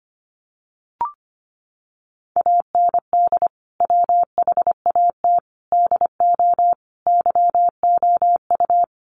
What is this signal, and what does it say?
Morse code, heard as a keyed tone, about 25 wpm, 700 Hz. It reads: AND WHAT DO YOU
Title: Sentences from Top 200 Words 25wpm